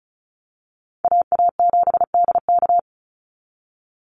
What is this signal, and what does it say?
Morse code, heard as a keyed tone, about 34 wpm, 700 Hz. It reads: AA7DK
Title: NA SOTA Chaser Callsigns 35wpm